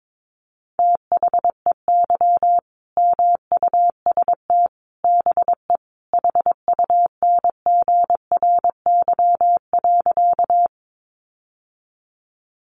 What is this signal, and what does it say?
Morse code, heard as a keyed tone, about 22 wpm, 700 Hz. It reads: THEY MUST BE HUNGRY.
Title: Sentences from Top 1400 Words 22wpm